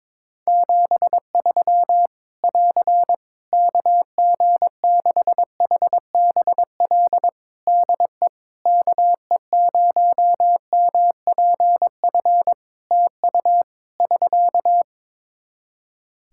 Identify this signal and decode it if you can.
Morse code, heard as a keyed tone, about 21 wpm, 700 Hz. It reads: 73 <AR> KG6HBL DE KE0MPF TU <SK>
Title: Practice QSO - Exchange 4 - Ending - 22wpm